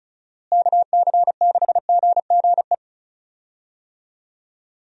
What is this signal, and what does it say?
Morse code, heard as a keyed tone, about 34 wpm, 700 Hz. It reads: KC6GGE
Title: USA Call Signs - 6 Characters - Repeated 3x - 35wpm